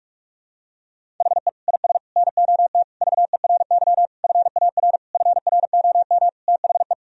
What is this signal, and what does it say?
Morse code, heard as a keyed tone, about 46 wpm, 700 Hz. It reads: HE IS NOT VERY FAR FROM THE
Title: Sentences from Top 1000 Words - Review Part 1 - 45wpm